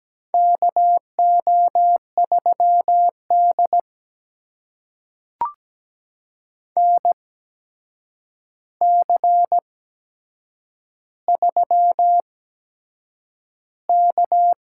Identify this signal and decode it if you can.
Morse code, heard as a keyed tone, about 17 wpm, spaced as for 7 wpm, 700 Hz. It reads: KO3D NC3K
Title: USA Call Signs - 4 Characters - 8x Character Spacing - 17wpm